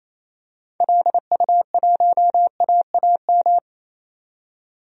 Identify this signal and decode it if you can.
Morse code, heard as a keyed tone, about 28 wpm, 700 Hz. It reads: LU1AAM
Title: DXCC-100 Call Signs - 6 Characters - 28wpm